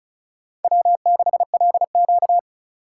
Morse code, heard as a keyed tone, about 34 wpm, 700 Hz. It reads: W6LQ
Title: USA Call Signs - 4 Characters - 4x Character Spacing - 35wpm